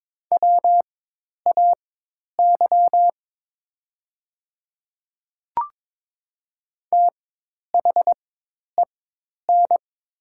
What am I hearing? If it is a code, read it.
Morse code, heard as a keyed tone, about 22 wpm, spaced as for 10 wpm, 700 Hz. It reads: WAY THEN
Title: 4x Character Spacing - Top 100 Words - 22wpm